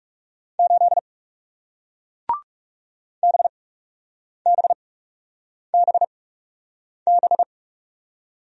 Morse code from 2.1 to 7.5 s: BBBB

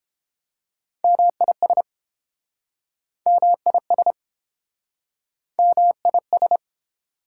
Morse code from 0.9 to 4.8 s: MIS MIS

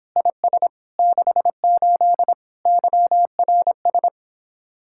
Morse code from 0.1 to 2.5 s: IS 68

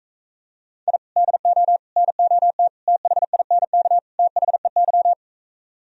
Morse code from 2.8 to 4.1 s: THINK

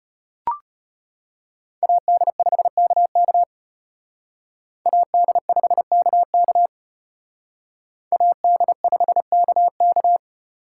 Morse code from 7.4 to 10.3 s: AD5KK

